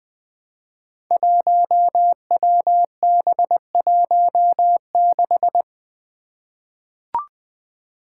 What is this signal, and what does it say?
Morse code, heard as a keyed tone, about 20 wpm, 700 Hz. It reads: 1WB16
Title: Five Challenging Letter-Number Combinations 20wpm